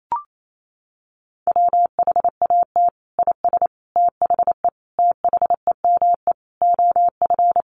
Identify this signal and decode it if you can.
Morse code, heard as a keyed tone, about 28 wpm, 700 Hz. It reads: WHAT IS THE THEME OF
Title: Sentences from Top 1100 Words 28wpm